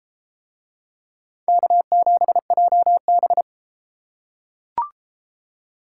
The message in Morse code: K7JB

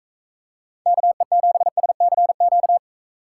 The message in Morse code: KE7SCQ

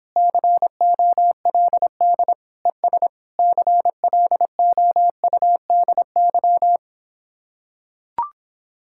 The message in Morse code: COLD ES CLOUDY

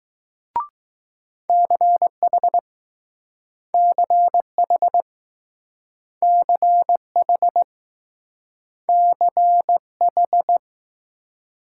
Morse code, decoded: CH CH CH CH

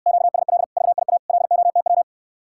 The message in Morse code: QSL VIA LOTW

